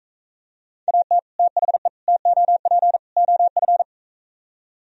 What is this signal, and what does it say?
Morse code, heard as a keyed tone, about 44 wpm, 700 Hz. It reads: AT THE TOP OF